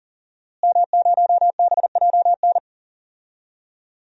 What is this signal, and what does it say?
Morse code, heard as a keyed tone, about 40 wpm, 700 Hz. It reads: M0BJN
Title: DXCC-100 Call Signs - 5 Characters - Repeated 3x - 40wpm